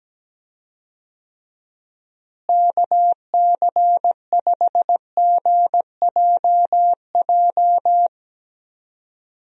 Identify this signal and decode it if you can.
Morse code, heard as a keyed tone, about 17 wpm, 700 Hz. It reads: KC5GJJ
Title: USA Call Signs - 6 Characters - Repeated 2x - 17wpm